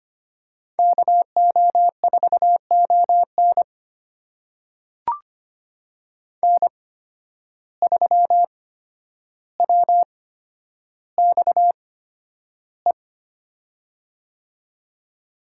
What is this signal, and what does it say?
Morse code, heard as a keyed tone, about 25 wpm, spaced as for 11 wpm, 700 Hz. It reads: KO4ON N3WXE
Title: USA Call Signs - 8x Character Spacing - 25wpm